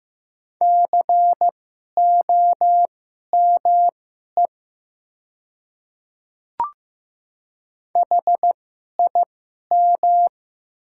Morse code, heard as a keyed tone, about 15 wpm, spaced as for 11 wpm, 700 Hz. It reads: COME HIM
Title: 2x Character Spacing - Top 100 Words - 15wpm